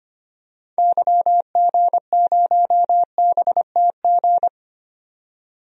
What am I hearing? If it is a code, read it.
Morse code, heard as a keyed tone, about 25 wpm, 700 Hz. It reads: YG0BTG
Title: DXCC-100 Call Signs - 6 Characters - 25wpm